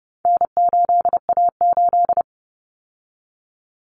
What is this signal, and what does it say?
Morse code, heard as a keyed tone, about 30 wpm, 700 Hz. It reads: N8A8
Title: Common Cut Numbers - 4-Digit 30wpm